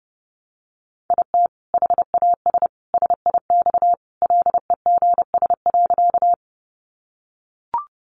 Morse code, heard as a keyed tone, about 30 wpm, 700 Hz. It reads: IT HAS SIX LEGS.
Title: Additional Easy Phrases 30wpm